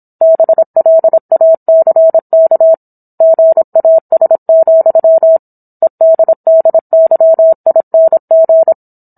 Morse code, heard as a keyed tone, about 27 wpm, 650 Hz. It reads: BLACK GAS, EDDYING